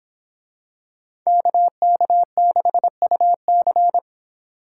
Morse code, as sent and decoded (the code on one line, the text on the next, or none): -.- -.- -.... ..- -.-.
KK6UC